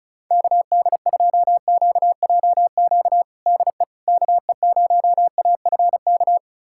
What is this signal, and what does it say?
Morse code, heard as a keyed tone, about 34 wpm, 700 Hz. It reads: KD2QJQ DE KE0AFK
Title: USA Callsign DE Callsign 35wpm